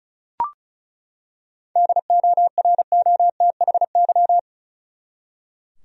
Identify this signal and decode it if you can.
Morse code, heard as a keyed tone, about 34 wpm, 700 Hz. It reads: DOROTHY
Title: USA Common Names - 35wpm